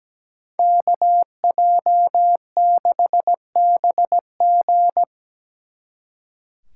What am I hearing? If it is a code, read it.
Morse code, heard as a keyed tone, about 19 wpm, 700 Hz. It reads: KJ6BG
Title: Speed Racing - USA 2x2 Call Signs 17wpm